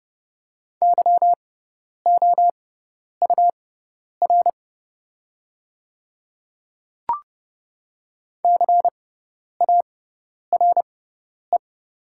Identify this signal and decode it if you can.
Morse code, heard as a keyed tone, about 29 wpm, spaced as for 10 wpm, 700 Hz. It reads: YOUR CARE
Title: 6x Character Spacing - Top 300 Words - 30wpm